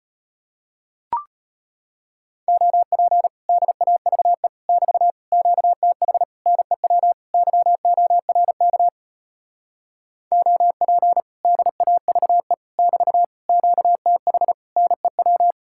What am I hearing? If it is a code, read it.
Morse code, heard as a keyed tone, about 33 wpm, 700 Hz. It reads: OP DAVE <BT> QTH NEW YORK OP DAVE <BT> QTH NEW